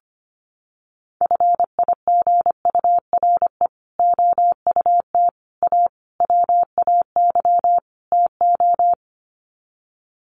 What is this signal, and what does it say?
Morse code, heard as a keyed tone, about 25 wpm, 700 Hz. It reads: FIGURE OUT A WAY TO
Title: Sets of 5 Words Encore2 25wpm